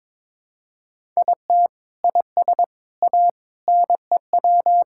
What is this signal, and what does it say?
Morse code, heard as a keyed tone, about 22 wpm, 700 Hz. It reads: IT IS A NEW
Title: Sentences from Top 1000 Words - Review Part 5 - 22wpm